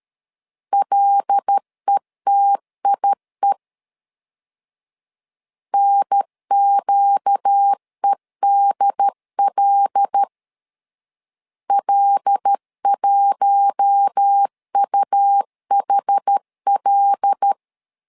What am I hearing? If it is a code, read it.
Morse code, heard as a keyed tone, about 13 wpm, 800 Hz. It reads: LETIE NQEDL L1UHL
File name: LEZIONI / 5 LBVC / 5 L 12wpm 1913